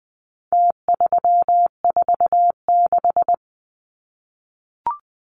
Morse code, heard as a keyed tone, about 20 wpm, 700 Hz. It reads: T346
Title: Numbers - Contest Style - 4-Digit 20wpm